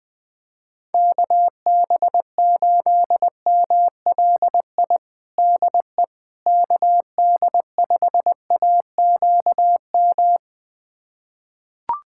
Morse code, heard as a keyed tone, about 20 wpm, 700 Hz. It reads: KB8MLI DE KD5AQM
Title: USA Callsign DE Callsign 20wpm